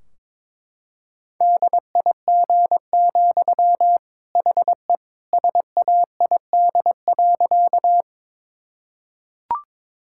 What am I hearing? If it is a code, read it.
Morse code, heard as a keyed tone, about 22 wpm, 700 Hz. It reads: DIG, HE SAID.
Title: Sentences from Top 1600 Words 22wpm